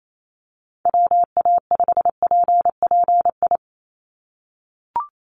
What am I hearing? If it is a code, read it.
Morse code, heard as a keyed tone, about 28 wpm, 700 Hz. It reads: WA5PPI
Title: USA Call Signs 28wpm